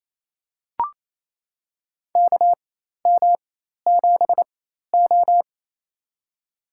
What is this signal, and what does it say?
Morse code, heard as a keyed tone, about 27 wpm, spaced as for 13 wpm, 700 Hz. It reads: KM7O
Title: USA Call Signs - 4x Character Spacing - 28wpm